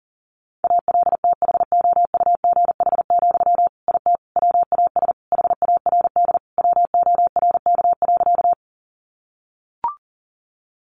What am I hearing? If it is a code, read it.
Morse code, heard as a keyed tone, about 37 wpm, 700 Hz. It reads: ALTHOUGH, IT WAS HARD WORK.
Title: Sentences from Top 600 Words 40wpm